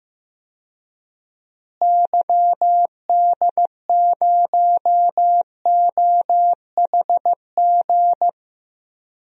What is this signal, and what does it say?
Morse code, heard as a keyed tone, about 15 wpm, 700 Hz. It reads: YD0OHG
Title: DXCC-100 Call Signs - 6 Characters - Repeated 2x - 15wpm